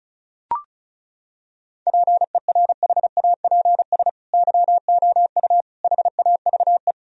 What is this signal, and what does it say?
Morse code, heard as a keyed tone, about 34 wpm, 700 Hz. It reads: PERHAPS YOU HAVE
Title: Sentences from Top 1000 Words - Review Part 1 - 35wpm